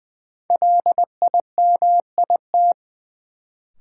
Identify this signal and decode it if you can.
Morse code, heard as a keyed tone, about 20 wpm, 700 Hz. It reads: LIMIT
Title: Sentences from Top 1500 Words - New Word Drill Alt - 20wpm